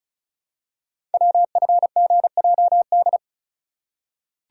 Morse code, heard as a keyed tone, about 36 wpm, 700 Hz. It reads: WFGJD
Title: Code Groups - 5 Letter:Numbers - 35wpm